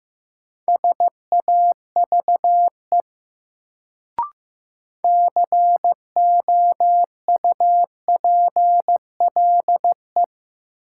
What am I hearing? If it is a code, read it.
Morse code, heard as a keyed tone, about 15 wpm, 700 Hz. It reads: SAVE COUPLE
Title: Sentences from Top 1100 Words - New Word Drill Alt - 15wpm